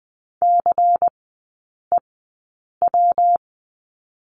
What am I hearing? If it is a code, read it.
Morse code, heard as a keyed tone, about 20 wpm, spaced as for 8 wpm, 700 Hz. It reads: CEW
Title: 114 Lesson 19e - Letter w - Three Characters 20wpm